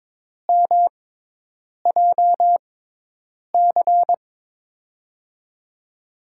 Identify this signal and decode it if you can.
Morse code, heard as a keyed tone, about 22 wpm, spaced as for 8 wpm, 700 Hz. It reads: MJC
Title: USA Call Signs - 5 Characters - 6x Character Spacing - 22wpm